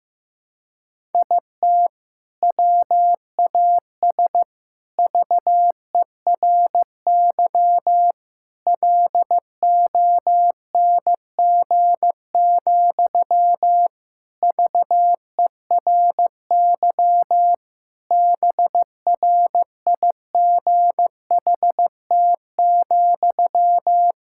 Morse code, heard as a keyed tone, about 15 wpm, 700 Hz. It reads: IT WAS VERY LONG, VERY BRIGHT,
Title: Sentences from Top 1600 Words 15wpm